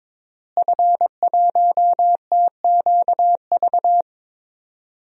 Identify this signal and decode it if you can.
Morse code, heard as a keyed tone, about 22 wpm, 700 Hz. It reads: F1TQV
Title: DXCC-100 Call Signs - 5 Characters - Repeated 2x - 22wpm